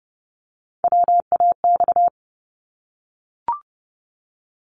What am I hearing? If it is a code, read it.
Morse code, heard as a keyed tone, about 30 wpm, 700 Hz. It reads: WAX
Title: Multi-letter Phonemes 30wpm